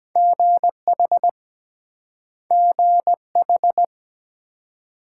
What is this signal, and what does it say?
Morse code, heard as a keyed tone, about 22 wpm, 700 Hz. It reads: GH GH